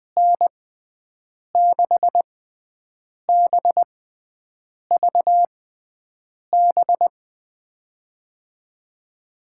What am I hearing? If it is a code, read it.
Morse code, heard as a keyed tone, about 20 wpm, spaced as for 7 wpm, 700 Hz. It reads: N6BVB
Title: USA Call Signs - 6x Character Spacing - 20wpm